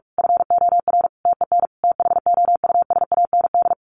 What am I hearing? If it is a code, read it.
Morse code, heard as a keyed tone, about 44 wpm, 700 Hz. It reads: FOR TEN THOUSAND
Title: A Sound Of Thunder 45wpm